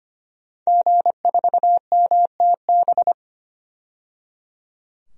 Morse code, heard as a keyed tone, about 25 wpm, 700 Hz. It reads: G4MTB
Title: DXCC-100 Call Signs - 5 Characters - 25wpm